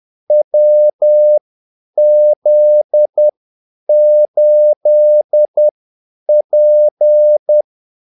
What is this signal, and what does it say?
Morse code, heard as a keyed tone, about 10 wpm, spaced as for 8 wpm, 600 Hz. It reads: WZ8P
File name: Callsigns-10-8wpm